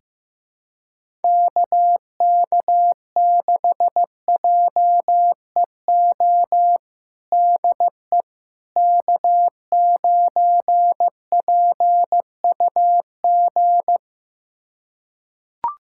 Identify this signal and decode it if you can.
Morse code, heard as a keyed tone, about 15 wpm, 700 Hz. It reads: KK6JEO DE K9PUG
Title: USA Callsign DE Callsign 15wpm